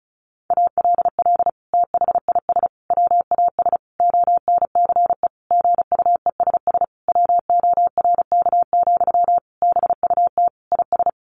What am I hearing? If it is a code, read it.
Morse code, heard as a keyed tone, about 34 wpm, 700 Hz. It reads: ALL THIS WAS ONCE GUESS WORK, BUT IS
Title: Sentences from Top 1500 Words 35wpm